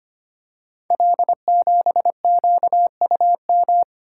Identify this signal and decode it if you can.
Morse code, heard as a keyed tone, about 25 wpm, 700 Hz. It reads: L7QUM
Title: Code Groups - 5 Letter:Numbers - 25wpm